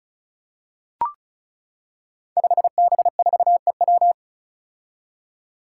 Morse code, heard as a keyed tone, about 34 wpm, 700 Hz. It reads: 5B4EW
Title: Five Challenging Letter-Number Combinations 35wpm